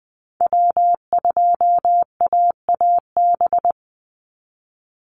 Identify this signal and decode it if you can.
Morse code, heard as a keyed tone, about 20 wpm, 700 Hz. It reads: W2AAB
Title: Call Signs Repeated 2x - 20wpm